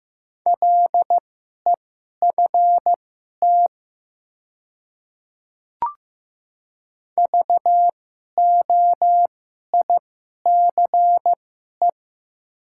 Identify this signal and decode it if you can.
Morse code, heard as a keyed tone, about 15 wpm, spaced as for 11 wpm, 700 Hz. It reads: LEFT VOICE
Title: 2x Character Spacing - Top 500 Words - 15wpm